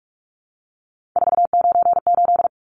Morse code, heard as a keyed tone, about 44 wpm, 700 Hz. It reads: 498
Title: Numbers - 3-Digit 45wpm